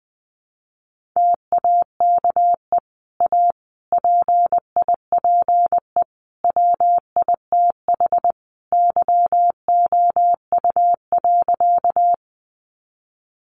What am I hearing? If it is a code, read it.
Morse code, heard as a keyed tone, about 20 wpm, 700 Hz. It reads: TAKE A PIPE WITH YOU.
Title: Sentences from Top 1500 Words 20wpm